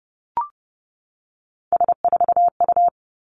Morse code, heard as a keyed tone, about 30 wpm, 700 Hz. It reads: S4U
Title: Three Challenging Letter-Number Combinations 30wpm